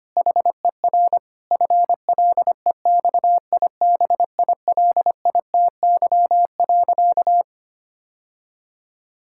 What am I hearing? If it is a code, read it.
Morse code, heard as a keyed tone, about 25 wpm, 700 Hz. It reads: HER FLEXIBILITY.
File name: practice / 25wpm / Difficult Sentences with Hints - 25wpm